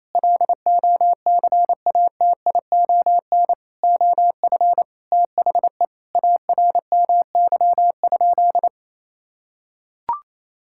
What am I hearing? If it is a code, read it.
Morse code, heard as a keyed tone, about 28 wpm, 700 Hz. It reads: LOCATION OF THE ARMY?
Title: Sentences from Top 1000 Words - Review Part 6 - 28wpm